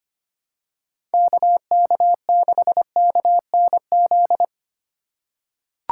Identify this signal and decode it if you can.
Morse code, heard as a keyed tone, about 25 wpm, 700 Hz. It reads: KK6KNZ
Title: USA Call Signs - 4x Character Spacing - 25wpm